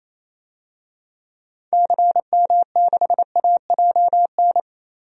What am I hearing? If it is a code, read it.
Morse code, heard as a keyed tone, about 28 wpm, 700 Hz. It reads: CM6AJN
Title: DXCC-100 Call Signs - 6 Characters - Repeated 3x - 28wpm